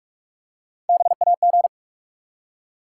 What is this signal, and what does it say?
Morse code, heard as a keyed tone, about 44 wpm, 700 Hz. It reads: BAG
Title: Three Letter Words - 45wpm